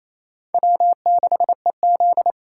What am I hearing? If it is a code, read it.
Morse code, heard as a keyed tone, about 35 wpm, 700 Hz. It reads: W6EZ
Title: Speed Racing - USA 1x2 Call Signs 28wpm